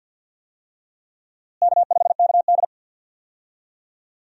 Morse code, heard as a keyed tone, about 51 wpm, 700 Hz. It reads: K5KD
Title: USA Call Signs - 4 Characters - Repeated 2x - 50wpm